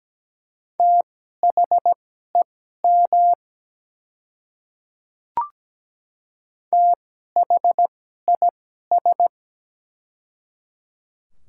Morse code, heard as a keyed tone, about 17 wpm, spaced as for 12 wpm, 700 Hz. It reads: THEM THIS
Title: 2x Character Spacing - Top 100 Words - 17wpm